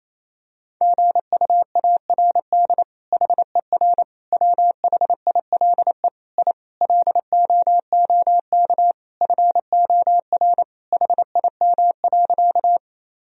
Morse code, heard as a keyed tone, about 27 wpm, 700 Hz. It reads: GUARD HER WHILE I LOOK FOR HIM.
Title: Sentences from Top 1000 Words - Review Part 4 - 28wpm